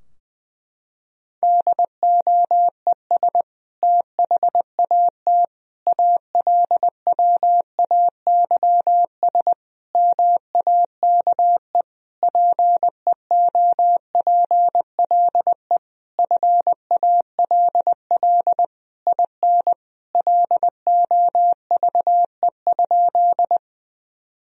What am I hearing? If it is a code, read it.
Morse code, heard as a keyed tone, about 20 wpm, 700 Hz. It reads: DOES THAT ALWAYS MAKE PEOPLE FALL IN LOVE?
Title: Sentences from Top 1000 Words - Review Part 2 - 20wpm